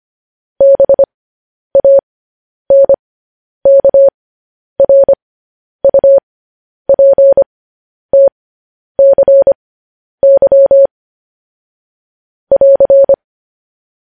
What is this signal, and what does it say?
Morse code, heard as a keyed tone, about 25 wpm, spaced as for 10 wpm, 550 Hz. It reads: BANKRUPTCY <AR>
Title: News Headlines at 10